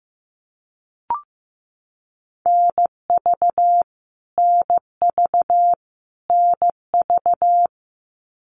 Morse code, heard as a keyed tone, about 15 wpm, 700 Hz. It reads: NV NV NV
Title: US States Repeated 3x 15wpm